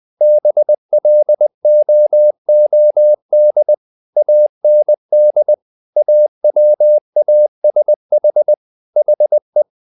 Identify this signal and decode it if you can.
Morse code, heard as a keyed tone, about 20 wpm, 600 Hz. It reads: BLOOD AND AWASH HE
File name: the old man and the sea part 2 20wpm